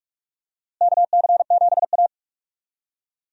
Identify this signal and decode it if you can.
Morse code, heard as a keyed tone, about 46 wpm, 700 Hz. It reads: KC7A
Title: USA Call Signs - 4 Characters - Repeated 2x - 45wpm